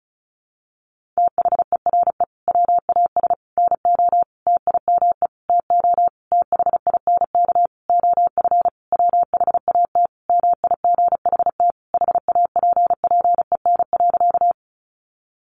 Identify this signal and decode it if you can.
Morse code, heard as a keyed tone, about 34 wpm, 700 Hz. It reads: THERE WAS NO TIME TO THINK OF WHAT MIGHT HAPPEN.
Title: Sentences from Top 400 Words 35wpm